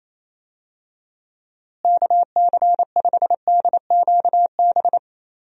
Morse code, heard as a keyed tone, about 28 wpm, 700 Hz. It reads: KC5DQB
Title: USA Call Signs - 6 Characters - Repeated 3x - 28wpm